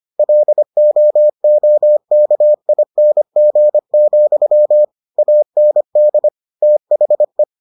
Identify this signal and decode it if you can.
Morse code, heard as a keyed tone, about 25 wpm, 600 Hz. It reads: LOOKING, AND THE